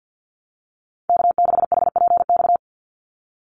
Morse code, heard as a keyed tone, about 50 wpm, 700 Hz. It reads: K6HPX
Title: NA SOTA Activator Callsigns 50wpm